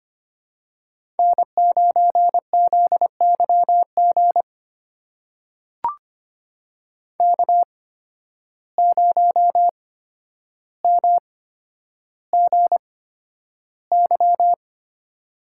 Morse code, heard as a keyed tone, about 26 wpm, spaced as for 11 wpm, 700 Hz. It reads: N9ZYG K0MGY
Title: USA Call Signs - 5 Characters - 8x Character Spacing - 25wpm